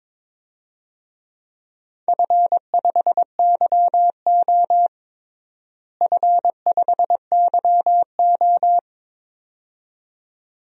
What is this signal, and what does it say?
Morse code, heard as a keyed tone, about 22 wpm, 700 Hz. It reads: F5YO F5YO
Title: DXCC-100 Call Signs - 4 Characters - Repeated 2x - 22wpm